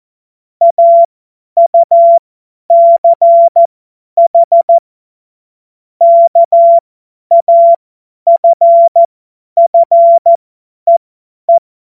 Morse code, heard as a keyed tone, about 14 wpm, spaced as for 10 wpm, 680 Hz. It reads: AUCH KAFFEE
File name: Morsekurs / Witze / Witz 11. Mai 2021, 50 F70